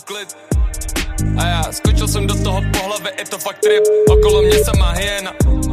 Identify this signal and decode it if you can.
Ringback tone (Europe, 425 Hz), with noise.